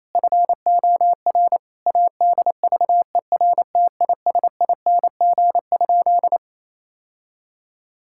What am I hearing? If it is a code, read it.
Morse code, heard as a keyed tone, about 28 wpm, 700 Hz. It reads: FOR ADVERTISING?
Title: Sentences from Top 1000 Words - Review Part 6 - 28wpm